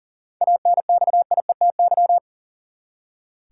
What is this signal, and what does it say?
Morse code, heard as a keyed tone, about 40 wpm, 700 Hz. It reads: ANXIETY
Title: Seven Letter Words - 40wpm